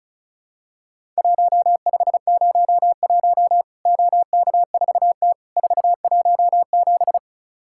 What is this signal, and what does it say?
Morse code, heard as a keyed tone, about 36 wpm, 700 Hz. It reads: 1501 OK4T 417
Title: Worked All Europe DX Contest - QTC Exchange - 35wpm